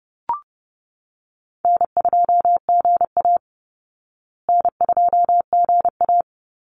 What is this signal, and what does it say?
Morse code, heard as a keyed tone, about 30 wpm, 700 Hz. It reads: N2GA N2GA
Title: Easy Call Signs Repeated 3x - 30wpm